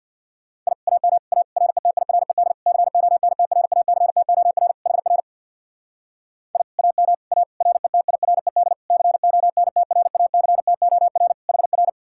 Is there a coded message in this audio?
Yes: Morse code from 0.7 to 11.9 s, heard as a keyed tone, about 64 wpm, 700 Hz. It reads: I AM A RETIRED CONTRACTOR HR I AM A RETIRED CONTRACTOR HR